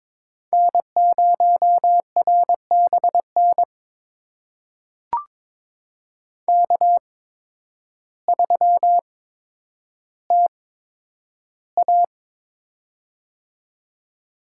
Morse code, heard as a keyed tone, about 22 wpm, spaced as for 10 wpm, 700 Hz. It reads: N0RBN K3TA
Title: USA Call Signs - 8x Character Spacing - 22wpm